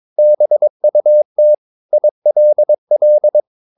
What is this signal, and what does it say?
Morse code, heard as a keyed tone, about 22 wpm, 600 Hz. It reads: BUT ILL
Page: Romanen Never af Ken Follet på engelsk.